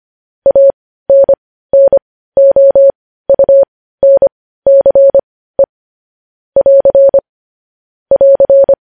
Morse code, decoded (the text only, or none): ANNOUNCE <AR> <AR>